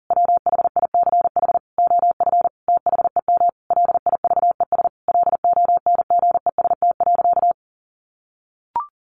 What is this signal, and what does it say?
Morse code, heard as a keyed tone, about 40 wpm, 700 Hz. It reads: WHICH OF THEM LIVES LONGEST.
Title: A Sound Of Thunder 40wpm